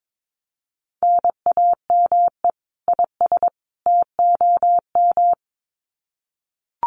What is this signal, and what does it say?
Morse code, heard as a keyed tone, about 22 wpm, 700 Hz. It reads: NAME IS TOM
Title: Common QSO Phrases 22wpm